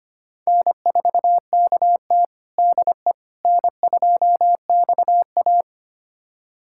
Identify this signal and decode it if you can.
Morse code, heard as a keyed tone, about 25 wpm, 700 Hz. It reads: N4KT DE N2XA
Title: USA Callsign DE Callsign - 4 Characters - 25wpm